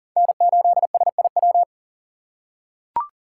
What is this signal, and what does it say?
Morse code, heard as a keyed tone, about 40 wpm, 690 Hz. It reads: N8SIW
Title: USA Call Signs - 5 Characters - 6x Character Spacing - 40wpm